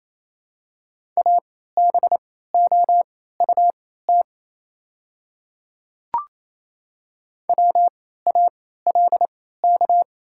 Morse code, heard as a keyed tone, about 28 wpm, spaced as for 16 wpm, 700 Hz. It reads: ABOUT WALK
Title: 3x Character Spacing - Top 300 Words - 28wpm